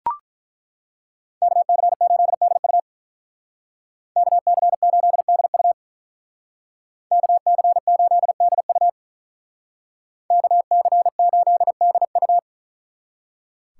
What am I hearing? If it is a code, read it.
Morse code, heard as a keyed tone, about 43 wpm, 700 Hz. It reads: KC8DU KC8DU KC8DU KC8DU
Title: Speed Racing - USA 2x2 Call Signs 35wpm